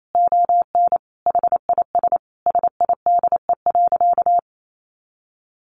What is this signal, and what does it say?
Morse code, heard as a keyed tone, about 28 wpm, 700 Hz. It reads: ON HIS SIDE.
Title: A Sound Of Thunder 28wpm